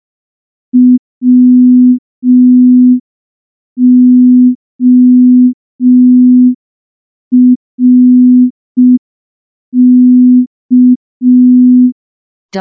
Morse code, heard as a keyed tone, about 5 wpm, 250 Hz. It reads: WORK
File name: morse / morse-training-05WPM